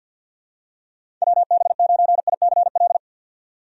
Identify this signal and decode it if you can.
Morse code, heard as a keyed tone, about 45 wpm, 700 Hz. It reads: WB9ICL
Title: USA Call Signs - 6 Characters - 50wpm